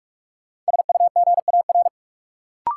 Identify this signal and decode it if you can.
Morse code, heard as a keyed tone, about 46 wpm, 700 Hz. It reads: SUGAR
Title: Sentences from Top 1200 Words - New Word Drill Alt - 45wpm